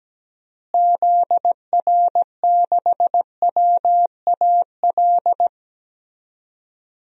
Morse code, heard as a keyed tone, about 17 wpm, 700 Hz. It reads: ZR6WAL
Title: DXCC-100 Call Signs - 6 Characters - Repeated 3x - 17wpm